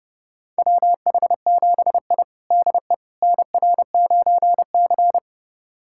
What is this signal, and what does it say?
Morse code, heard as a keyed tone, about 31 wpm, 700 Hz. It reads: WH7I DE NR9C
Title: USA Callsign DE Callsign - 4 Characters - 30wpm